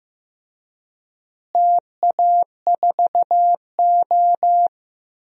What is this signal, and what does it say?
Morse code, heard as a keyed tone, about 16 wpm, 700 Hz. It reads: TA4O
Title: DXCC-100 Call Signs - 4 Characters - Repeated 3x - 15wpm